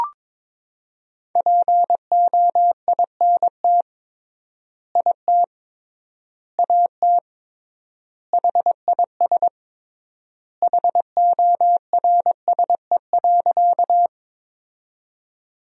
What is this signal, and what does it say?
Morse code, heard as a keyed tone, about 22 wpm, 700 Hz. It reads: POINT IT AT HIS HORSE.
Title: 3x Word-Spacing - Sentences from Top 300 Words - 22wpm